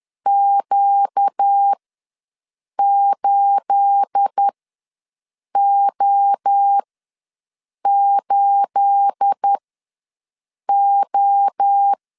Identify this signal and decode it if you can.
Morse code, heard as a keyed tone, about 10 wpm, spaced as for 6 wpm, 800 Hz. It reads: Q8O8O
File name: LEZIONI / 9 28 ws 10wpm 4124